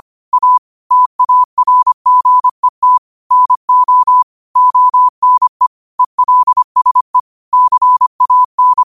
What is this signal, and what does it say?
Morse code, heard as a keyed tone, about 25 wpm, 1000 Hz. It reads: A TARGET NO ONE ELSE CAN